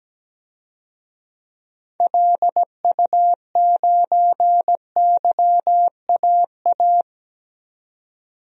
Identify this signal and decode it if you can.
Morse code, heard as a keyed tone, about 17 wpm, 700 Hz. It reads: LU9YAA
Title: DXCC-100 Call Signs - 6 Characters - 17wpm